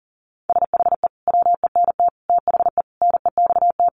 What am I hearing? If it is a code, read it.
Morse code, heard as a keyed tone, about 37 wpm, 700 Hz. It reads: SHE WENT THE NEXT
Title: Sentences from Top 1500 Words 40wpm